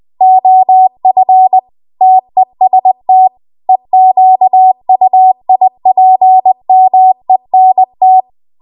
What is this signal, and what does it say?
Morse code, heard as a keyed tone, about 20 wpm, 750 Hz. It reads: OF TEST EQUIPMENT
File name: Archive / 20 WPM / 20WPM